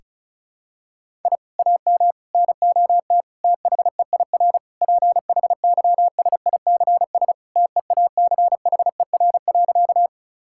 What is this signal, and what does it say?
Morse code, heard as a keyed tone, about 36 wpm, 700 Hz. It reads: I AM NOT THEIR PHYSICS TEACHER.
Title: Sentences from Top 1000 Words - Review Part 6 - 35wpm